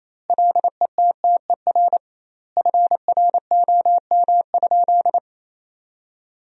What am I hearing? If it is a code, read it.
Morse code, heard as a keyed tone, about 28 wpm, 700 Hz. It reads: LETTER FROM?
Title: 2x Word-Spacing - Sentences from Top 300 Words - 28wpm